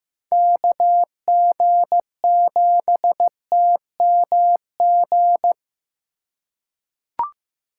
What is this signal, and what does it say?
Morse code, heard as a keyed tone, about 15 wpm, 700 Hz. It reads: KG7TMG
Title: USA Call Signs - 6 Characters - 2x Character Spacing - 15wpm